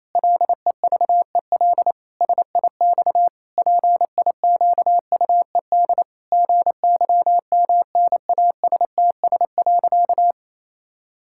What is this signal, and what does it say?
Morse code, heard as a keyed tone, about 28 wpm, 700 Hz. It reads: LEVEL SIX PIQUED GYMNASTS.
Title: Pangram Challenge v2 - 28wpm